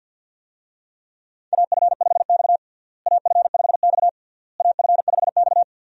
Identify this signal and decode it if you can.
Morse code, heard as a keyed tone, about 52 wpm, 700 Hz. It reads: AF5X AF5X AF5X
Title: USA Call Signs - 4 Characters - Repeated 3x - 50wpm